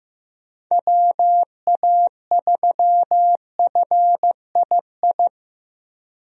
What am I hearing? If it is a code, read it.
Morse code, heard as a keyed tone, about 15 wpm, 700 Hz. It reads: WA3FII
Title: USA Call Signs - 6 Characters - Repeated 2x - 15wpm